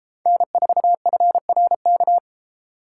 Morse code, heard as a keyed tone, about 33 wpm, 700 Hz. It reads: N4FRK